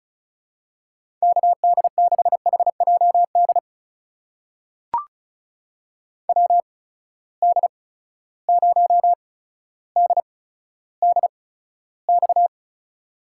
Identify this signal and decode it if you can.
Morse code, heard as a keyed tone, about 36 wpm, spaced as for 15 wpm, 700 Hz. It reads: KD6HJD WD0DDX
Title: USA Call Signs - 6 Characters - 8x Character Spacing - 35wpm